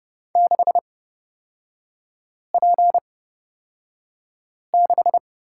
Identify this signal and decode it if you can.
Morse code, heard as a keyed tone, about 30 wpm, spaced as for 5 wpm, 690 Hz. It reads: 6P6
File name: Lesson 38e - Letter q - Three Characters 30wpm